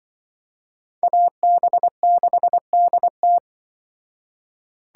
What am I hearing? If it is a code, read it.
Morse code, heard as a keyed tone, about 24 wpm, 700 Hz. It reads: AB6DT
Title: Speed Racing - USA 2x2 Call Signs 20wpm